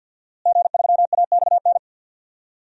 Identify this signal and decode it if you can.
Morse code, heard as a keyed tone, about 52 wpm, 700 Hz. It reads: G3AXN
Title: DXCC-100 Call Signs - 5 Characters - Repeated 3x - 50wpm